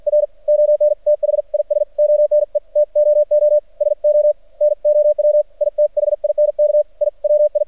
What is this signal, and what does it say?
Morse code, heard as a keyed tone, about 56 wpm, spaced as for 48 wpm, 600 Hz. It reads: R ON THIS ONE TOO SO NOW I THINK I WI
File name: archived / CRSnet